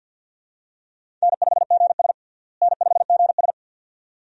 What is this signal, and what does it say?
Morse code, heard as a keyed tone, about 51 wpm, 700 Hz. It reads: N5GS N5GS
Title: USA Call Signs - 4 Characters - Repeated 2x - 50wpm